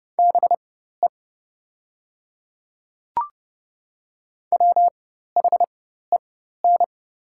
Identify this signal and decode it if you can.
Morse code, heard as a keyed tone, about 30 wpm, spaced as for 14 wpm, 700 Hz. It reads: BE WHEN
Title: 4x Character Spacing - Top 200 Words - 30wpm